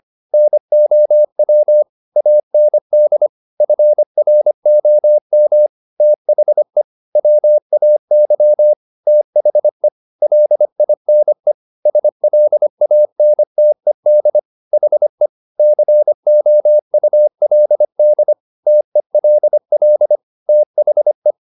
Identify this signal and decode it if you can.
Morse code, heard as a keyed tone, about 25 wpm, 600 Hz. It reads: NOW AND FROM THE WAY THE LINE SLANTED HE COULD TELL THE